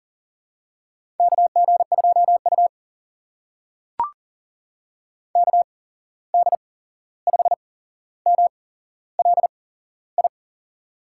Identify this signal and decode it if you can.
Morse code, heard as a keyed tone, about 40 wpm, spaced as for 15 wpm, 700 Hz. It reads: KG2U KD5MLI
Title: USA Call Signs - 8x Character Spacing - 40wpm